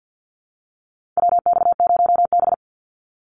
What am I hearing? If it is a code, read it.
Morse code, heard as a keyed tone, about 50 wpm, 700 Hz. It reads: WX0B
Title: Call Signs Repeated 2x - 50wpm